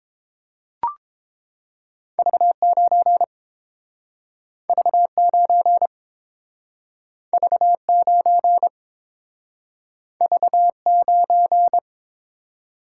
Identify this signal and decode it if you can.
Morse code, heard as a keyed tone, about 30 wpm, 700 Hz. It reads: V9 V9 V9 V9